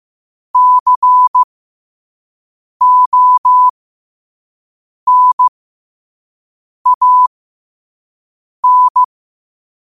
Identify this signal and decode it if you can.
Morse code, heard as a keyed tone, about 15 wpm, spaced as for 5 wpm, 1000 Hz. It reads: CONAN
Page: Quotes for Wed, 13 Aug 2025 in Morse Code at 5 words per minute.